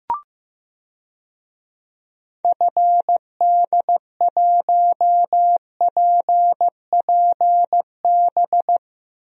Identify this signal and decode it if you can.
Morse code, heard as a keyed tone, about 16 wpm, 700 Hz. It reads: FD1PPB
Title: DXCC-100 Call Signs - 6 Characters - Repeated 3x - 15wpm